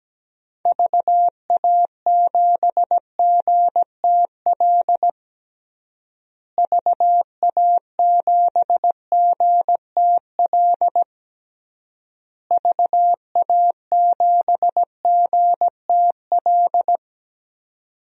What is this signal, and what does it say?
Morse code, heard as a keyed tone, about 17 wpm, 700 Hz. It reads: VA7GTL VA7GTL VA7GTL
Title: DXCC-100 Call Signs - 6 Characters - Repeated 3x - 17wpm